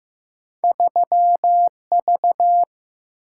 Morse code, heard as a keyed tone, about 15 wpm, 700 Hz. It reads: 3V
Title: Single-Letter Single-Number 15wpm